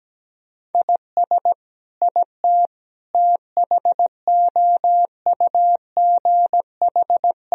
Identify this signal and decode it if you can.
Morse code, heard as a keyed tone, about 17 wpm, 700 Hz. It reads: IS IT THOUGH
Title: Sentences from Top 1000 Words - Review Part 2 - 17wpm